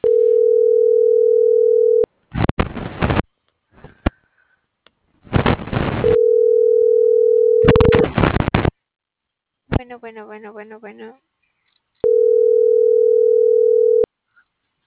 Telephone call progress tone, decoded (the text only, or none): ringback tone